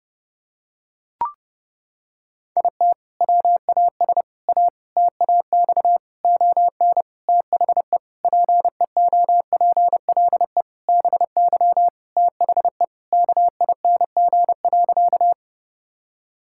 Morse code, heard as a keyed tone, about 30 wpm, 700 Hz. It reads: IT WAS A TAX ON THE PEOPLE BY THE KING.
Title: Sentences from Top 1000 Words - Review Part 5 - 30wpm